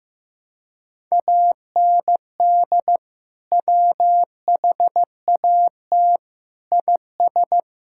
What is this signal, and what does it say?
Morse code, heard as a keyed tone, about 15 wpm, 700 Hz. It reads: AND WHAT IS
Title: Sentences from Top 1000 Words - Review Part 6 - 15wpm